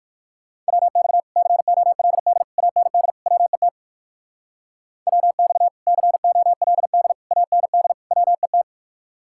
Morse code, heard as a keyed tone, about 59 wpm, 700 Hz. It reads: WX COLD AND WET WX COLD AND WET